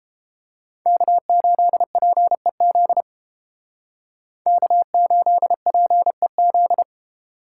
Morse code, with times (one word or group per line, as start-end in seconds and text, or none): 0.9-3.0 K8PEZ
4.5-6.8 K8PEZ